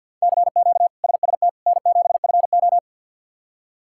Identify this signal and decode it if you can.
Morse code, heard as a keyed tone, about 52 wpm, 700 Hz. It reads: CQ SST N7FO
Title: K1USN SST QSO Practice - 50wpm